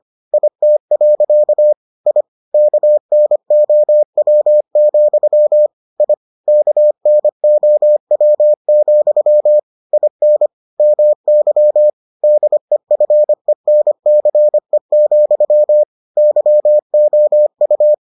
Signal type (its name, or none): Morse code